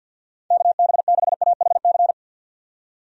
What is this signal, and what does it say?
Morse code, heard as a keyed tone, about 52 wpm, 700 Hz. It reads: KB6AHC
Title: USA Call Signs - 6 Characters - 50wpm